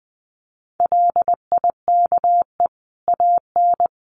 Morse code, heard as a keyed tone, about 20 wpm, 700 Hz. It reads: LIKE AN
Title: A Sound Of Thunder 20wpm